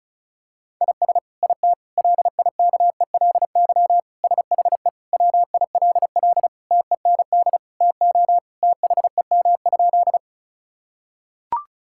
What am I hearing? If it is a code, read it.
Morse code, heard as a keyed tone, about 36 wpm, 690 Hz. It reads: IS IT LIKELY SHE WILL TEND TO THEM?
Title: Sentences from Top 1000 Words - Review Part 5 - 35wpm